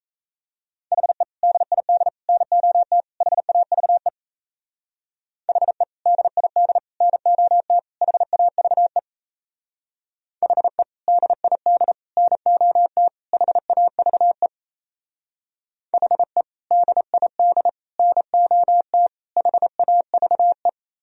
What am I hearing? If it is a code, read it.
Morse code, heard as a keyed tone, about 35 wpm, 700 Hz. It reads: HE DID NOT HAVE HE DID NOT HAVE HE DID NOT HAVE HE DID NOT HAVE
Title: Speed-Racing - Sets of 4 Words 28wpm